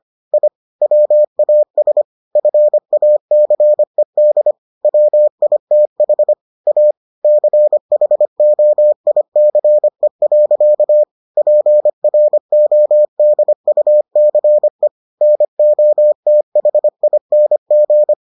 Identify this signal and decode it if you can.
Morse code, heard as a keyed tone, about 25 wpm, 600 Hz. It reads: I WAS FACED WITH A CHOICE. PRODUCE NOTHING